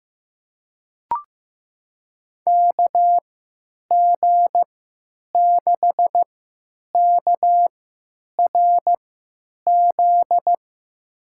Morse code, heard as a keyed tone, about 15 wpm, spaced as for 9 wpm, 700 Hz. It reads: KG6KRZ
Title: USA Call Signs - 6 Characters - 3x Character Spacing - 15wpm